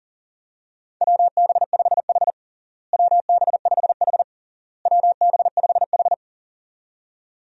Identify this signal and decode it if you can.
Morse code, heard as a keyed tone, about 40 wpm, 700 Hz. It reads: WB5H WB5H WB5H
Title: USA Call Signs - 4 Characters - Repeated 3x - 40wpm